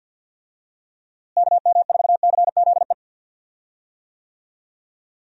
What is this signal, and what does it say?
Morse code, heard as a keyed tone, about 47 wpm, 700 Hz. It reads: KM4CZE
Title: USA Call Signs - 6 Characters - Repeated 3x - 50wpm